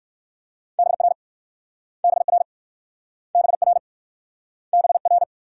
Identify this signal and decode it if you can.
Morse code, heard as a keyed tone, about 59 wpm, 700 Hz. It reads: BR BR BR BR